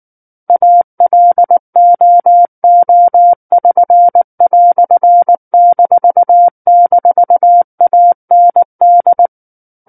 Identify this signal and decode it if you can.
Morse code, heard as a keyed tone, about 19 wpm, 700 Hz. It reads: ALOO<SN>"--AND
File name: Book One / 10-Chapter 10 - 19 WPM